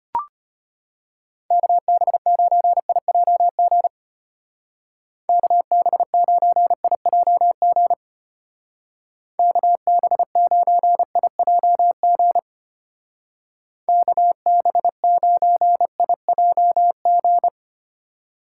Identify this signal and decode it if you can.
Morse code, heard as a keyed tone, about 32 wpm, 700 Hz. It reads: KB9IJG KB9IJG KB9IJG KB9IJG